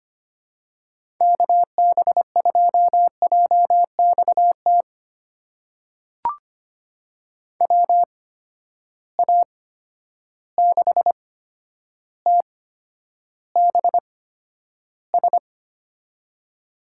Morse code, heard as a keyed tone, about 25 wpm, spaced as for 11 wpm, 700 Hz. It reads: KB2JXT WA6TBS
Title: USA Call Signs - 6 Characters - 8x Character Spacing - 25wpm